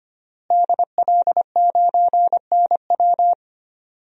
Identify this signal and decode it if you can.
Morse code, heard as a keyed tone, about 25 wpm, 700 Hz. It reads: DL9NW
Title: DXCC-100 Call Signs - 5 Characters - Repeated 2x - 25wpm